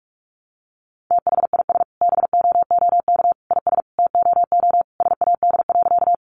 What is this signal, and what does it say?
Morse code, heard as a keyed tone, about 44 wpm, 700 Hz. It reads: THIS BOOK IS TOO SAD.
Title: Additional Easy Phrases 45wpm